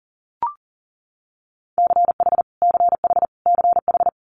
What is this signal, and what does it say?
Morse code, heard as a keyed tone, about 37 wpm, 690 Hz. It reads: CH CH CH
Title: Multi-letter Phonemes Repeated 3x 40wpm